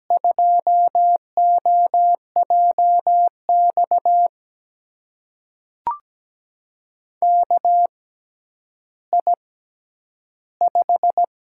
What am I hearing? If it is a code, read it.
Morse code, heard as a keyed tone, about 17 wpm, spaced as for 10 wpm, 700 Hz. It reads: 2OJX KI5
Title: USA Call Signs - 6 Characters - 6x Character Spacing - 17wpm